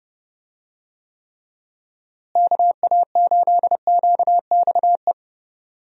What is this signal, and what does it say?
Morse code, heard as a keyed tone, about 30 wpm, 700 Hz. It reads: KA8QXE
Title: USA Call Signs - 6 Characters - Repeated 3x - 30wpm